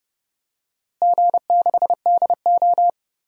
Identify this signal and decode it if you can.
Morse code, heard as a keyed tone, about 30 wpm, 700 Hz. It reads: G6DO
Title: DXCC-100 Call Signs - 4 Characters - Repeated 2x - 30wpm